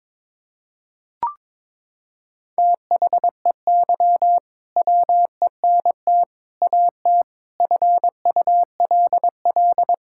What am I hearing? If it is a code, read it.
Morse code, heard as a keyed tone, about 22 wpm, 700 Hz. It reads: THEY WENT AT FULL
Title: Sentences from Top 1000 Words - Review Part 5 - 22wpm